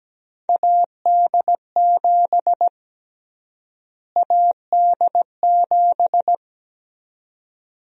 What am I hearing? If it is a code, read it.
Morse code, heard as a keyed tone, about 17 wpm, 700 Hz. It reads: AD7 AD7
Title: USA Call Sign Prefixes - Repeated 2x - 17wpm